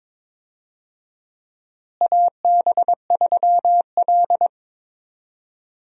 Morse code, heard as a keyed tone, about 21 wpm, 700 Hz. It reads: AB3L
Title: USA Call Signs - 4 Characters - Repeated 2x - 22wpm